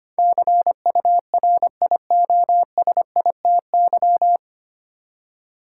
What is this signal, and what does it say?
Morse code, heard as a keyed tone, about 25 wpm, 700 Hz. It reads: CURIOSITY
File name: Nine Letter Words - 25wpm